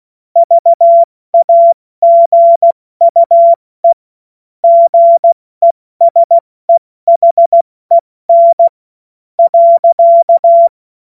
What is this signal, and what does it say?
Morse code, heard as a keyed tone, about 16 wpm, 680 Hz. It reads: VAGUE GESEHEN .
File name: Morsekurs / Witze / Witz 02 April 2024 70 F80